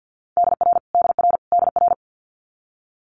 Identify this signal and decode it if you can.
Morse code, heard as a keyed tone, about 50 wpm, 700 Hz. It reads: DR DR DR
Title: Multi-letter Phonemes Repeated 3x 50wpm